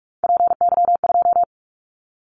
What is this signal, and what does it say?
Morse code, heard as a keyed tone, about 44 wpm, 700 Hz. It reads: PY2
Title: Callsign Prefixes for Top 50 DXCC - 45wpm